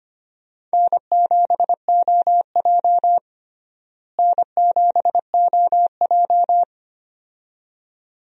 Morse code, heard as a keyed tone, about 25 wpm, 700 Hz. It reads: N7OJ N7OJ
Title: USA Call Signs - 4 Characters - Repeated 2x - 25wpm